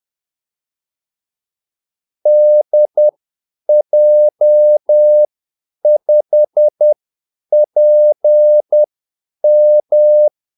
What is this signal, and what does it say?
Morse code, heard as a keyed tone, about 10 wpm, spaced as for 8 wpm, 600 Hz. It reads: DJ5PM
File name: Callsigns-10-8wpm